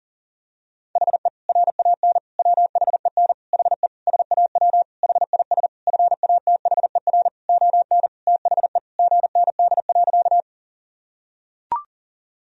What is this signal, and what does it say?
Morse code, heard as a keyed tone, about 37 wpm, 700 Hz. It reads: HE RAN WHEN HE SAW HIS FATHER ON THE GND.
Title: Sentences from Top 500 Words and Abbreviations 40wpm